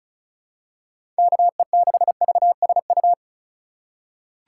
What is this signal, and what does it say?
Morse code, heard as a keyed tone, about 36 wpm, 690 Hz. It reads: KE6VSU
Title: USA Call Signs - 4x Character Spacing - 35wpm